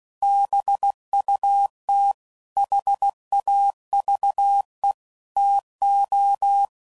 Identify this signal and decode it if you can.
Morse code, heard as a keyed tone, about 16 wpm, 800 Hz. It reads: BUT HAVE TO
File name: CW Code Groups @15WPM / QSOs / kc7haa